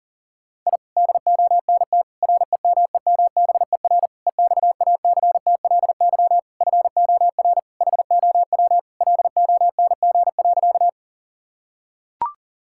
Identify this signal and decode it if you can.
Morse code, heard as a keyed tone, about 40 wpm, 700 Hz. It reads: I DONT REMEMBER EXACTLY FOR HOW LONG.
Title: Sentences from Top 1000 Words - Review Part 6 - 40wpm